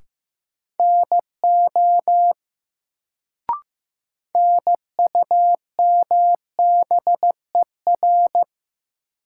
Morse code, heard as a keyed tone, about 15 wpm, 700 Hz. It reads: NO NUMBER
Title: Sets of 2 Words - New Word Drill - 15wpm